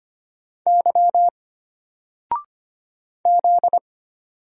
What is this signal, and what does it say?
Morse code, heard as a keyed tone, about 25 wpm, 700 Hz. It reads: Y Z
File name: Single-Letter-Number-25wpm